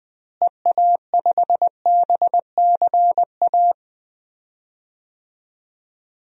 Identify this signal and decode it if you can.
Morse code, heard as a keyed tone, about 20 wpm, 700 Hz. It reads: EA5BCA
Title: DXCC-100 Call Signs - 6 Characters - Repeated 2x - 20wpm